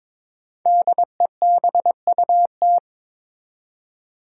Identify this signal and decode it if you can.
Morse code, heard as a keyed tone, about 22 wpm, 700 Hz. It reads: DEBUT
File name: 5 Letter Words - First 3 Letters Given - 22wpm